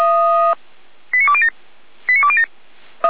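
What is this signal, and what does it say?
Five-tone selective call (ZVEI), heard as quick runs of five tones, tones of about 70 ms. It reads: 89207 89207
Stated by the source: ZVEI five-tone selective call received by SDR.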